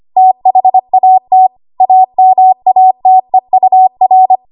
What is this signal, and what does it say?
Morse code, heard as a keyed tone, about 25 wpm, 750 Hz. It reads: THAT AMATEUR
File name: Morse / 25WPM